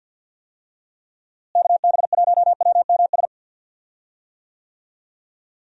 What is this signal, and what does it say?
Morse code, heard as a keyed tone, about 46 wpm, 700 Hz. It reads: KB1WMS
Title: USA Call Signs - 6 Characters - Repeated 2x - 50wpm